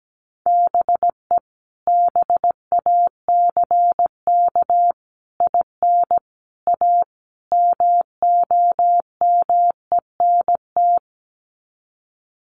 Morse code, heard as a keyed tone, about 17 wpm, 700 Hz. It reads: BE BACK IN A MOMENT
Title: Sets of 5 Words 17wpm